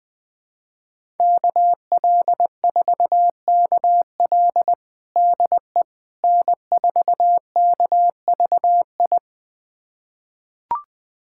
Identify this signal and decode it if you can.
Morse code, heard as a keyed tone, about 20 wpm, 700 Hz. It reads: KL4KL DE N4KVI
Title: USA Callsign DE Callsign - 5 Characters - 20wpm